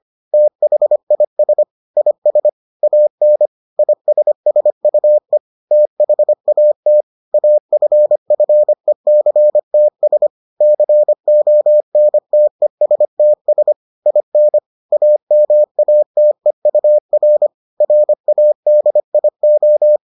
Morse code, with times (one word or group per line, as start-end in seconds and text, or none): 0.3-1.6 THIS
2.0-2.5 IS
2.8-3.5 AN
3.8-5.4 ISSUE
5.7-7.0 THAT
7.3-10.3 AFFECTS
10.6-13.7 CONTESTS
14.1-14.6 IN
14.9-17.5 AMATEUR
17.8-20.0 RADIO